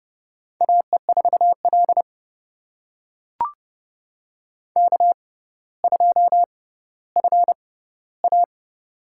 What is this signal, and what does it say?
Morse code, heard as a keyed tone, about 30 wpm, spaced as for 15 wpm, 700 Hz. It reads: AE4L K2FA
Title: USA Call Signs - 6x Character Spacing - 30wpm